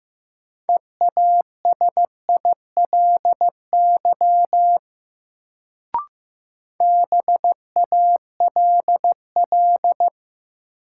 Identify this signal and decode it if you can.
Morse code, heard as a keyed tone, about 15 wpm, 700 Hz. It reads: EASILY BALL
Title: Sentences from Top 900 Words - New Word Drill - 15wpm